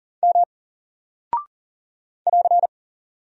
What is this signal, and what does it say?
Morse code, heard as a keyed tone, about 40 wpm, 700 Hz. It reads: M <AR>
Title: Single Character (Letter:Number:Punctuation:Prosign) 40wpm